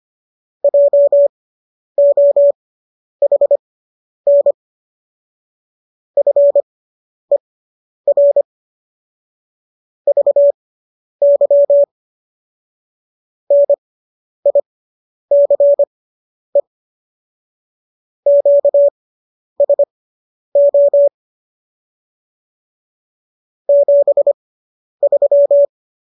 Morse code, heard as a keyed tone, about 25 wpm, spaced as for 10 wpm, 570 Hz. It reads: JOHN FER VY NICE QSO 73